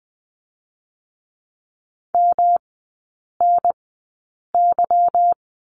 Morse code, heard as a keyed tone, about 20 wpm, spaced as for 8 wpm, 700 Hz. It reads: MNY
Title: 152 Lesson 25e - Letter p - Three Characters 20wpm